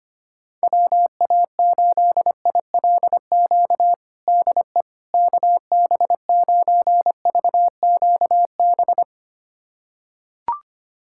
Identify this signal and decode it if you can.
Morse code, heard as a keyed tone, about 25 wpm, 700 Hz. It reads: WA8ILQ DE KB9VQB
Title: USA Callsign DE Callsign 25wpm